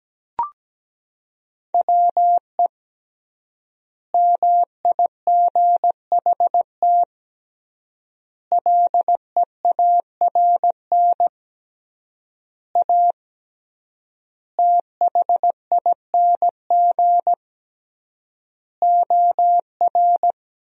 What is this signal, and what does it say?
Morse code, heard as a keyed tone, about 17 wpm, 700 Hz. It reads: WE MIGHT LEARN A THING OR
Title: 3x Word-Spacing - Sentences from Top 300 Words - 17wpm